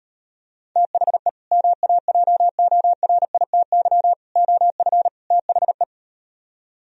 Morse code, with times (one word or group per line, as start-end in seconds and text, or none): 0.8-1.3 THE
1.5-4.1 MAJORITY
4.4-5.1 OF
5.3-5.8 THE